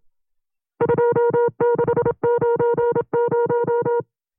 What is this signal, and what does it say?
Morse code, heard as a keyed tone, about 27 wpm, 470 Hz. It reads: 2690
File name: FreeFiles / CW-practice / 25WPM-side2